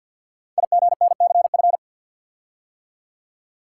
Morse code, heard as a keyed tone, about 50 wpm, 700 Hz. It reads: IGNCF
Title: Code Groups - 5 Letters - 50wpm